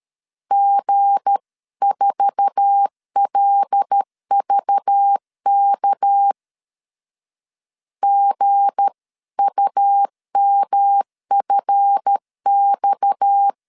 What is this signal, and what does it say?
Morse code, heard as a keyed tone, about 13 wpm, 800 Hz. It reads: G4LVK GUMFX
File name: CW / DET / 1 9 1500C S A 12wpm 3800